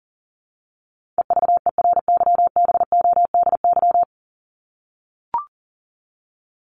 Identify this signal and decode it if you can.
Morse code, heard as a keyed tone, about 40 wpm, 700 Hz. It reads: EVERYBODY
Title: Top 1251-1500 Words 40wpm